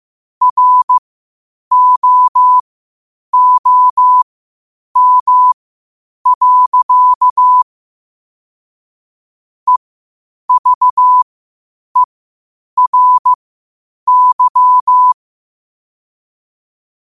Morse code, heard as a keyed tone, about 15 wpm, spaced as for 8 wpm, 1000 Hz. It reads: ROOM. EVERY
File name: qotdfiles / 8wpm